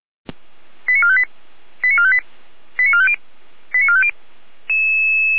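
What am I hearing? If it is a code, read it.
Five-tone selective call (ZVEI), heard as quick runs of five tones, tones of about 70 ms. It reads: 89458 89458 89450 89450
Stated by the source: ZVEI five-tone selective call received by SDR.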